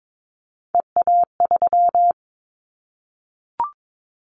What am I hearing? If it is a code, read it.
Morse code, heard as a keyed tone, about 22 wpm, 700 Hz. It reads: EA3
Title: Callsign Prefixes for Top 25 DXCC - Repeated 3x - 22wpm